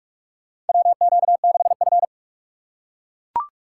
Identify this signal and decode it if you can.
Morse code, heard as a keyed tone, about 43 wpm, 700 Hz. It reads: WQ6F
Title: USA Call Signs - 4 Characters - 6x Character Spacing - 45wpm